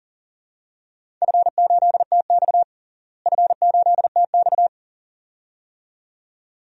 Morse code, heard as a keyed tone, about 40 wpm, 700 Hz. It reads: F8TX F8TX
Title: DXCC-100 Call Signs - 4 Characters - Repeated 2x - 40wpm